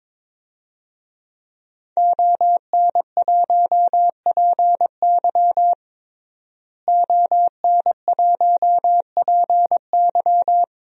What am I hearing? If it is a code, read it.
Morse code, heard as a keyed tone, about 21 wpm, 700 Hz. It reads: ON1PY ON1PY
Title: DXCC-100 Call Signs - 5 Characters - Repeated 2x - 22wpm